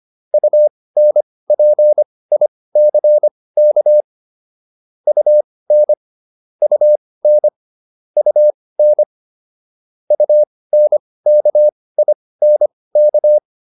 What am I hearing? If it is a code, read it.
Morse code, heard as a keyed tone, about 25 wpm, spaced as for 18 wpm, 600 Hz. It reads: UNPICK UN UN UN UNKINK